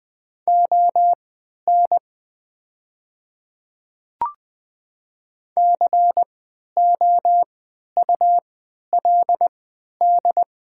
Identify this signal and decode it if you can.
Morse code, heard as a keyed tone, about 20 wpm, spaced as for 11 wpm, 700 Hz. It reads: ON COULD
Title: 3x Character Spacing - Top 100 Words - 20wpm